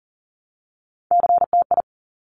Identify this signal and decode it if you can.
Morse code, heard as a keyed tone, about 42 wpm, 700 Hz. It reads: CTI
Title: Top Three Letter Combinations in Words 40wpm